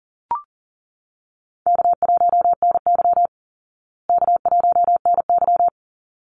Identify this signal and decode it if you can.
Morse code, heard as a keyed tone, about 40 wpm, 700 Hz. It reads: K1NY K1NY
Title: Easy Call Signs Repeated 2x - 40wpm